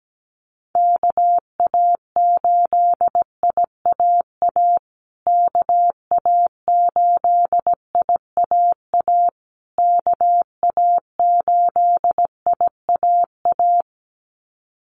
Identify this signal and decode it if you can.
Morse code, heard as a keyed tone, about 17 wpm, 700 Hz. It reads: KA8IAA KA8IAA KA8IAA
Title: USA Call Signs Repeated 3x 17wpm